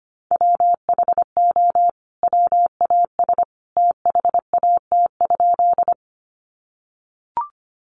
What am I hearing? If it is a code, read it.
Morse code, heard as a keyed tone, about 25 wpm, 700 Hz. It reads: WHO WAS THAT?
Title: Sentences from Top 100 Words 25wpm